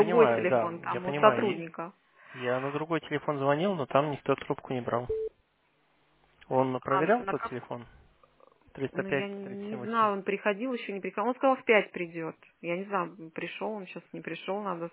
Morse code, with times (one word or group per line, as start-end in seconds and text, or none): none